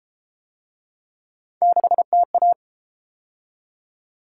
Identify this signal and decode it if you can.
Morse code, heard as a keyed tone, about 30 wpm, 700 Hz. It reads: 6TA